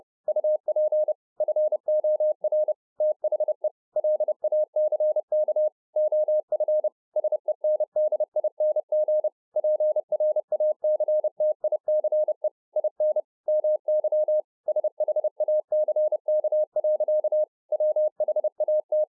Morse code, heard as a keyed tone, about 30 wpm, 610 Hz. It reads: UP FOR THE LACK OF SENDING PRACTICE IN MY SHACK. WHAT